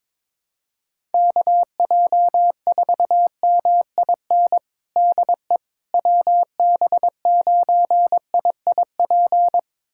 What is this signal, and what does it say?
Morse code, heard as a keyed tone, about 22 wpm, 700 Hz. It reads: KJ4MIN DE WB9IIP
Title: USA Callsign DE Callsign - 6 Characters - 22wpm